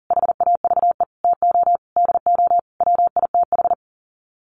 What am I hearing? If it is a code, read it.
Morse code, heard as a keyed tone, about 40 wpm, 700 Hz. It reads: HAVE TO DO WITH
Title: Sets of 5 Words Encore2 40wpm